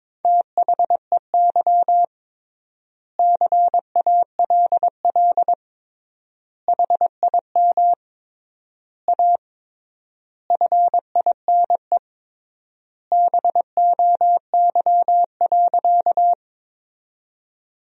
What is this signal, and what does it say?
Morse code, heard as a keyed tone, about 22 wpm, 700 Hz. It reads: THEY CALL HIM A FINE BOY.
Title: 3x Word-Spacing - Sentences from Top 500 Words - 22wpm